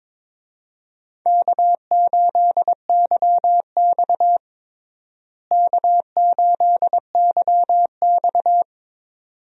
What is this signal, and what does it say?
Morse code, heard as a keyed tone, about 22 wpm, 700 Hz. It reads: K8YX K8YX
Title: USA Call Signs - 4 Characters - Repeated 2x - 22wpm